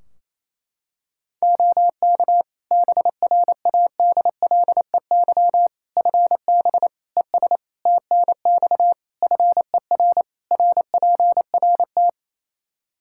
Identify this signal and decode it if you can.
Morse code, heard as a keyed tone, about 28 wpm, 700 Hz. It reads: OK BRADLEY FB ES TNX FER RPRT
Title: Practice QSO - Exchange 2 - Info - 28wpm